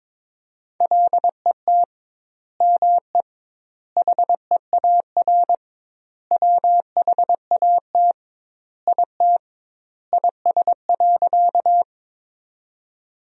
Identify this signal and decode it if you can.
Morse code, heard as a keyed tone, about 22 wpm, 700 Hz. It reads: LET ME HEAR WHAT IT IS.
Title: 2x Word-Spacing - Sentences from Top 300 Words - 22wpm